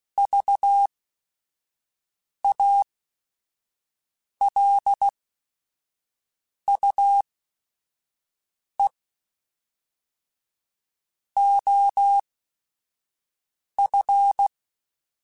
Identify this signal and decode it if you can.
Morse code, heard as a keyed tone, about 16 wpm, spaced as for 5 wpm, 800 Hz. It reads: VALUE OF
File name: Technical Subjects / Technical Subjects-mp3s / XC @5